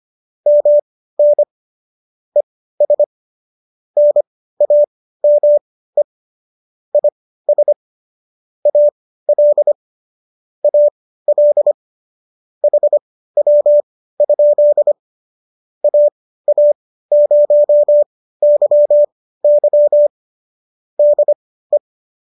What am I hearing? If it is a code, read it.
Morse code, heard as a keyed tone, about 25 wpm, spaced as for 15 wpm, 590 Hz. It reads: MN ES NAME IS AL AL HW? AA0YY DE
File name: qso213 15